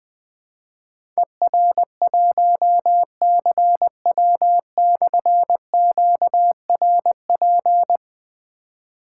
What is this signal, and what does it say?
Morse code, heard as a keyed tone, about 20 wpm, 700 Hz. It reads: ER1CW/QRP
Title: Difficult Call Signs 20wpm